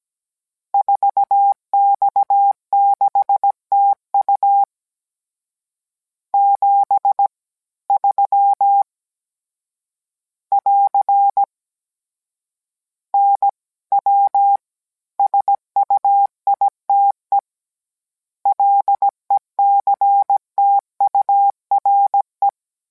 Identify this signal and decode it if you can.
Morse code, heard as a keyed tone, about 17 wpm, 800 Hz. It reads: 4X6TU 73 <AR> NW SUITE LECTURE